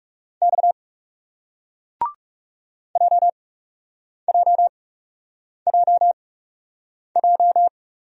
Morse code, heard as a keyed tone, about 38 wpm, spaced as for 8 wpm, 700 Hz. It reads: X JJJJ